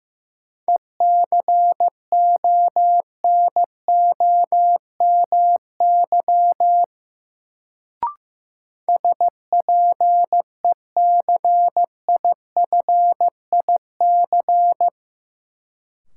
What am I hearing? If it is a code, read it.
Morse code, heard as a keyed tone, about 15 wpm, 700 Hz. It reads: ECONOMY SPECIFIC
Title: Sentences from Top 1000 Words - Review Part 3 - New Word Drill - 15wpm